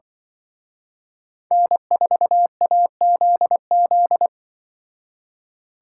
Morse code, heard as a keyed tone, about 24 wpm, 700 Hz. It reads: N4AZZ
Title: Speed Racing - USA 1x3 Call Signs 20wpm